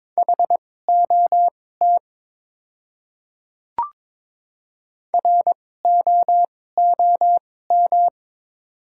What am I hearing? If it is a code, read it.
Morse code, heard as a keyed tone, about 22 wpm, spaced as for 16 wpm, 700 Hz. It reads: HOT ROOM
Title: 2x Character Spacing - Top 300 Words - 22wpm